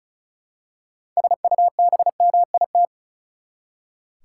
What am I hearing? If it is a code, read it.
Morse code, heard as a keyed tone, about 34 wpm, 700 Hz. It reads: SUBMIT
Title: Six Letter Words - 35wpm